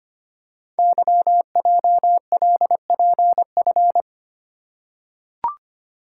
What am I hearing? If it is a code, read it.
Morse code, heard as a keyed tone, about 25 wpm, 700 Hz. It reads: YJLPF
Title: Code Groups - 5 Letters - 25wpm